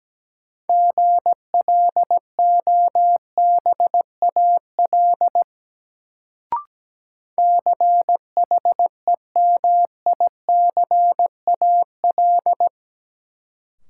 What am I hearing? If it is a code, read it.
Morse code, heard as a keyed tone, about 17 wpm, 700 Hz. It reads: GLOBAL CHEMICAL
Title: Sentences from Top 1200 Words - New Word Drill Alt - 17wpm